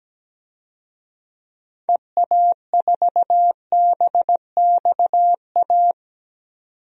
Morse code, heard as a keyed tone, about 17 wpm, 700 Hz. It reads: EA4BXA
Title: DXCC-100 Call Signs - 6 Characters - 17wpm